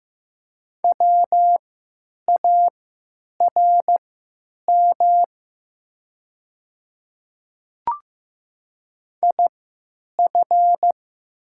Morse code, heard as a keyed tone, about 15 wpm, spaced as for 9 wpm, 700 Hz. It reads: WARM IF
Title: 3x Character Spacing - Top 500 Words - 15wpm